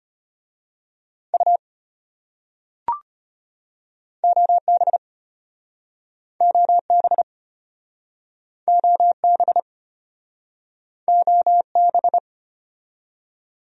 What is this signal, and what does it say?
Morse code, heard as a keyed tone, about 33 wpm, 700 Hz. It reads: U OB OB OB OB